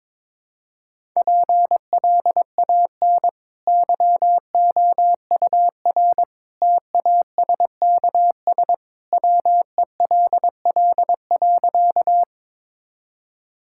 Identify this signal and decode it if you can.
Morse code, heard as a keyed tone, about 22 wpm, 700 Hz. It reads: PLAN YOUR TASKS WELL.
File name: practice / 22wpm / Easy Sentences with Hints - 22wpm